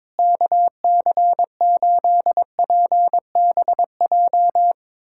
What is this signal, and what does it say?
Morse code, heard as a keyed tone, about 22 wpm, 700 Hz. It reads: KC8PBJ
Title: USA Callsign DE Callsign 22wpm